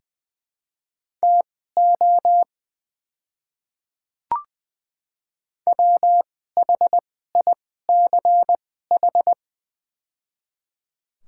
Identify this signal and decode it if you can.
Morse code, heard as a keyed tone, about 21 wpm, spaced as for 15 wpm, 700 Hz. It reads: TO WHICH
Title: 2x Character Spacing - Top 100 Words - 20wpm